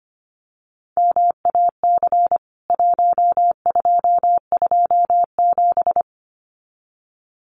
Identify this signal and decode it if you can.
Morse code, heard as a keyed tone, about 25 wpm, 690 Hz. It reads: MAC 1227
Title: CWT Exchange 25wpm